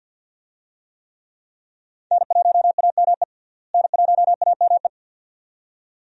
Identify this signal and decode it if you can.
Morse code, heard as a keyed tone, about 51 wpm, 700 Hz. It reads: N1AME N1AME
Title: USA Call Signs - 5 Characters - Repeated 2x - 50wpm